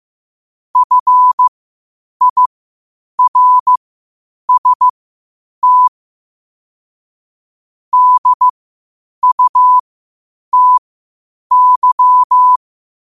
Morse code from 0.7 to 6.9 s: FIRST

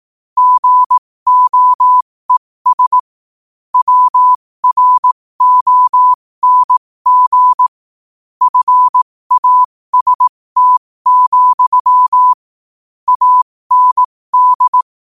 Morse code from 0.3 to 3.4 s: GOES

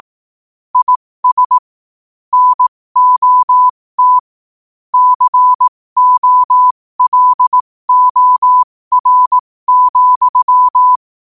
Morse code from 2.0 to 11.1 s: NOT COLOR,

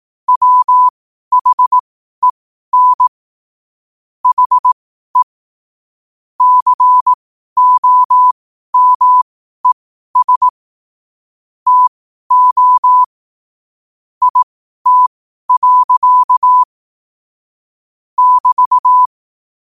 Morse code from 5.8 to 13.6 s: COMES TO

